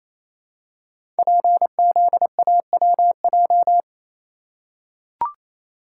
Morse code, heard as a keyed tone, about 27 wpm, 700 Hz. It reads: PZAWJ
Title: Code Groups - 5 Letter:Numbers - 28wpm